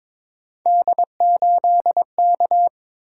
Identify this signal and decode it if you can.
Morse code, heard as a keyed tone, about 22 wpm, 700 Hz. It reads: D8K
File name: practice / 22wpm / Three Letter-Number Combinations - 22wpm